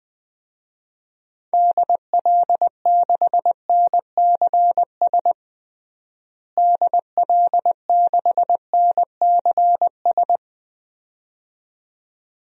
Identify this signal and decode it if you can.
Morse code, heard as a keyed tone, about 21 wpm, 700 Hz. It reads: DL6NCS DL6NCS
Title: DXCC-100 Call Signs - 6 Characters - Repeated 2x - 20wpm